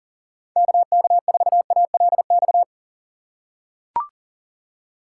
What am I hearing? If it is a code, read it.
Morse code, heard as a keyed tone, about 40 wpm, 700 Hz. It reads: KK4ALX
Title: USA Call Signs - 6 Characters - 6x Character Spacing - 40wpm